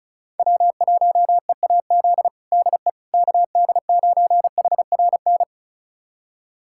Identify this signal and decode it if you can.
Morse code, heard as a keyed tone, about 36 wpm, 700 Hz. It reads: W1EAZ DE KD9HRN
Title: USA Callsign DE Callsign 35wpm